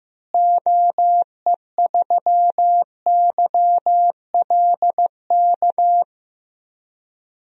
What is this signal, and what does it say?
Morse code, heard as a keyed tone, about 15 wpm, 700 Hz. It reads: OE3YLK
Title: DXCC-100 Call Signs - 6 Characters - Repeated 3x - 15wpm